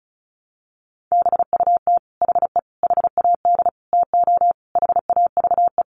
Morse code, heard as a keyed tone, about 34 wpm, 700 Hz. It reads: BUT HE HAD TO HAVE
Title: Sentences from Top 100 Words 35wpm